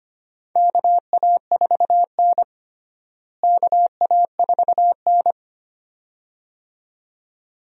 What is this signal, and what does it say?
Morse code, heard as a keyed tone, about 25 wpm, 700 Hz. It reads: KA4N KA4N
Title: USA Call Signs - 4 Characters - Repeated 2x - 25wpm